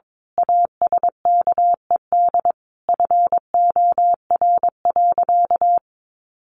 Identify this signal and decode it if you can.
Morse code, heard as a keyed tone, about 22 wpm, 700 Hz. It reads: ASKED FOR.
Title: A Sound Of Thunder 22wpm